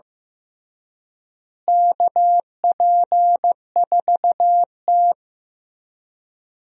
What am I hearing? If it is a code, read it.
Morse code, heard as a keyed tone, about 15 wpm, 700 Hz. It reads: KP4T
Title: DXCC-100 Call Signs - 4 Characters - Repeated 3x - 15wpm